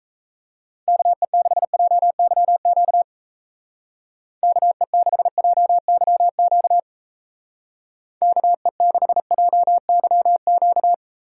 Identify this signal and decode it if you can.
Morse code, heard as a keyed tone, about 34 wpm, 700 Hz. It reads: KE6JYQ KE6JYQ KE6JYQ
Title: Speed Racing - USA Call Signs 28wpm